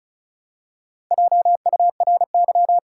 Morse code, heard as a keyed tone, about 36 wpm, 700 Hz. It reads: JURY
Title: Four Letter Words - 35wpm